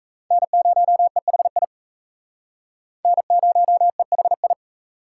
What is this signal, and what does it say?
Morse code, heard as a keyed tone, about 38 wpm, 700 Hz. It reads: N0EHI N0EHI